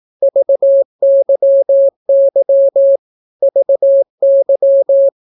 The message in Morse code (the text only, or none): VYY VY